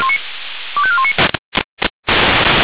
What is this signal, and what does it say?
Five-tone selective call (ZVEI), heard as quick runs of five tones, tones of about 70 ms. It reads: 25419